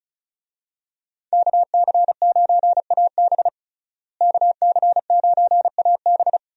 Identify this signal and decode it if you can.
Morse code, heard as a keyed tone, about 36 wpm, 700 Hz. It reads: KC9AB KC9AB
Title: USA Call Signs - 5 Characters - Repeated 3x - 35wpm